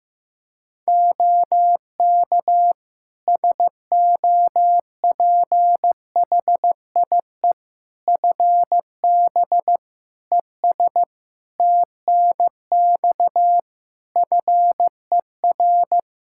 Morse code, heard as a keyed tone, about 16 wpm, 700 Hz. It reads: OK SOPHIE FB ES TNX FER
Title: Practice QSO - Exchange 2 - Info - 15wpm